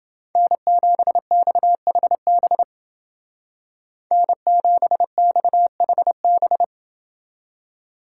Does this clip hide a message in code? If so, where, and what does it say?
Yes: Morse code from 0.4 to 6.6 s, heard as a keyed tone, about 29 wpm, 700 Hz. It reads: N7XHB N7XHB